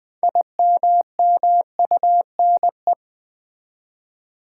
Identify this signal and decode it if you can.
Morse code, heard as a keyed tone, about 20 wpm, 700 Hz. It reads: IMMUNE
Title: Build A Word - 3 Letters At A Time - 6-Letter Words - 20wpm